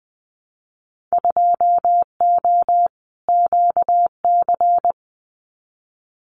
Morse code, heard as a keyed tone, about 20 wpm, 700 Hz. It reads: 2O QC
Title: Winter Field Day Exchanges 20wpm